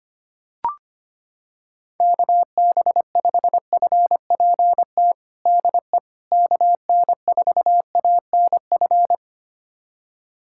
Morse code, heard as a keyed tone, about 25 wpm, 700 Hz. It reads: KB5FPT DE KN4ANF
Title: USA Callsign DE Callsign - 6 Characters - 25wpm